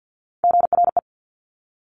Morse code, heard as a keyed tone, about 50 wpm, 700 Hz. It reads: GRE
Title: Top Three Letter Combinations in Words 50wpm